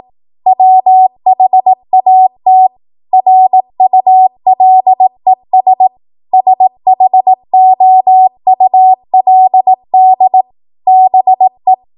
Morse code, heard as a keyed tone, about 18 wpm, 750 Hz. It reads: WHAT RULES SHOULD BE
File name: Morse / 18WPM